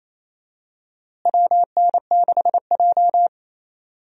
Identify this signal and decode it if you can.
Morse code, heard as a keyed tone, about 27 wpm, 700 Hz. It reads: WN6J
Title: USA Call Signs - 4 Characters - Repeated 2x - 28wpm